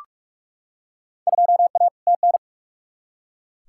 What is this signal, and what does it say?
Morse code, heard as a keyed tone, about 44 wpm, 700 Hz. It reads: 2A TN
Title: Field Day Exchanges Encore - 45wpm